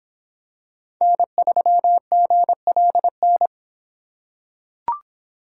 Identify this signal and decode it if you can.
Morse code, heard as a keyed tone, about 26 wpm, 700 Hz. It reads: N3GLN